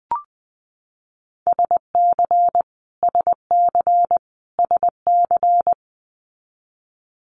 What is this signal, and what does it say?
Morse code, heard as a keyed tone, about 19 wpm, 700 Hz. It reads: SC SC SC
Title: Multi-letter Phonemes Repeated 3x 20wpm